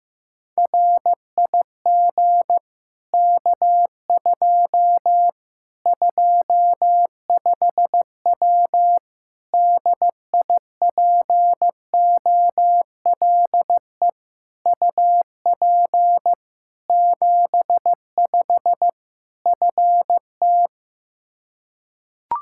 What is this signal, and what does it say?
Morse code, heard as a keyed tone, about 15 wpm, 700 Hz. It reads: RIG K2 25W DIPOLE UP 75 FT
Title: Practice QSO - Exchange 2 - Info - 15wpm